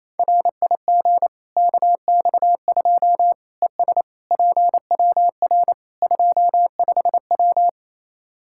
Morse code, heard as a keyed tone, about 27 wpm, 700 Hz. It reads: RIG KX2 ES PWR 25W
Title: Practice QSO - Exchange 2 - Info - 28wpm